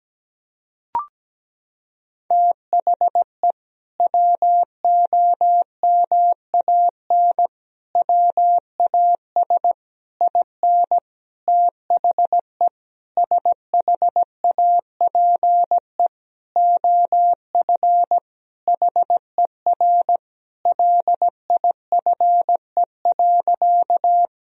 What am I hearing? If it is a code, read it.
Morse code, heard as a keyed tone, about 17 wpm, 700 Hz. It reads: THE WOMAN WAS IN THE SHAPE OF HER LIFE.
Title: Sentences from Top 1000 Words - Review Part 3 - 17wpm